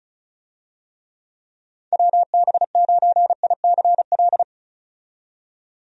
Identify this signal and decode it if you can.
Morse code, heard as a keyed tone, about 34 wpm, 700 Hz. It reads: WB9ICL
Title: USA Call Signs - 6 Characters - 35wpm